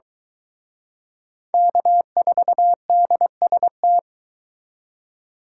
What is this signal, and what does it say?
Morse code, heard as a keyed tone, about 23 wpm, 700 Hz. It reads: K4DST